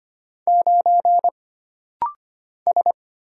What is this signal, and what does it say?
Morse code, heard as a keyed tone, about 25 wpm, 700 Hz. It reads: 9 S
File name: Single Character (Letter:Number:Punctuation:Prosign) - Rapid-Fire - 25wpm